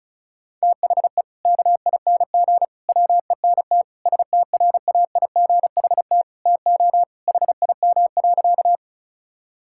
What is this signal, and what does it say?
Morse code, heard as a keyed tone, about 36 wpm, 700 Hz. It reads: THE KING WENT STRAIGHT TO HIM.
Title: Sentences from Top 1000 Words - Review Part 6 - 35wpm